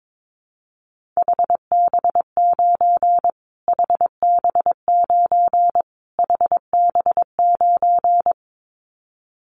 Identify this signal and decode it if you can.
Morse code, heard as a keyed tone, about 22 wpm, 700 Hz. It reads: HB9 HB9 HB9
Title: Callsign Prefixes for Top 25 DXCC - Repeated 3x - 22wpm